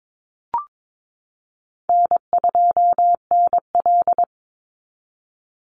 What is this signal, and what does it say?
Morse code, heard as a keyed tone, about 22 wpm, 700 Hz. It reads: N2NL
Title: Call Signs Repeated 2x - 22wpm